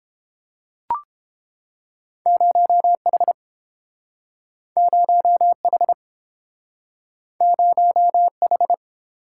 Morse code, heard as a keyed tone, about 29 wpm, 700 Hz. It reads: 0H 0H 0H